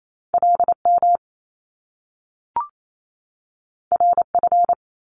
Morse code, heard as a keyed tone, about 28 wpm, 700 Hz. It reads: LM RF
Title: Top Two Letter Combinations in Words 28wpm